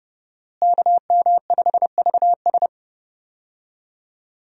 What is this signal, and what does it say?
Morse code, heard as a keyed tone, about 30 wpm, 700 Hz. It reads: KM5VS
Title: USA Call Signs - 5 Characters - Repeated 3x - 30wpm